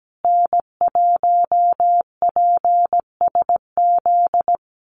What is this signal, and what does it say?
Morse code, heard as a keyed tone, about 17 wpm, 700 Hz. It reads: N1PSZ
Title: USA Call Signs Repeated 3x 17wpm